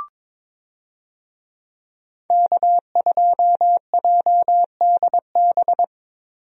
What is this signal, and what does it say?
Morse code, heard as a keyed tone, about 21 wpm, 700 Hz. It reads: K2JDB
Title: USA Call Signs - 5 Characters - Repeated 3x - 22wpm